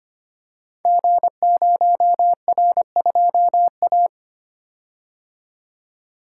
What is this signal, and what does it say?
Morse code, heard as a keyed tone, about 25 wpm, 700 Hz. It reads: G0R2A
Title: Code Groups - 5 Letter:Numbers - 25wpm